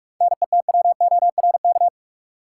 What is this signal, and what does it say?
Morse code, heard as a keyed tone, about 46 wpm, 700 Hz. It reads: NETWORK
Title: Sentences from Top 1000 Words - Review Part 7 - New Word Drill - 45wpm